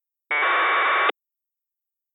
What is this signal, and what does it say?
Packet radio (AX.25, as in APRS).